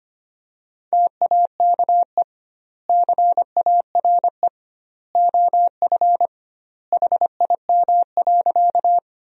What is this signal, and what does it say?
Morse code, heard as a keyed tone, about 25 wpm, 700 Hz. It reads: TAKE CARE OF HIM.
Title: 2x Word-Spacing - Sentences from Top 300 Words - 25wpm